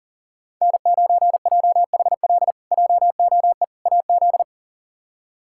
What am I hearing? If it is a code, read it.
Morse code, heard as a keyed tone, about 40 wpm, 700 Hz. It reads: N9JHL JOE AZ
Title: K1USN SST QSO Practice - 40wpm